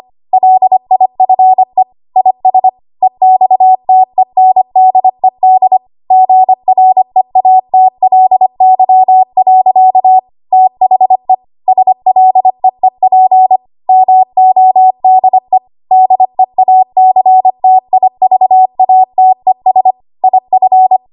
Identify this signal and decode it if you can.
Morse code, heard as a keyed tone, about 25 wpm, 750 Hz. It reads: LIFE IS EXTENDED GREATLY. THE SLEEP MODE DEACTIVATES IF
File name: Archive / 25 WPM / 25WPM